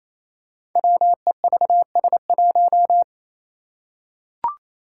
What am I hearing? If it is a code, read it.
Morse code, heard as a keyed tone, about 27 wpm, 700 Hz. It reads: WEVS1
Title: Five Challenging Letter-Number Combinations 28wpm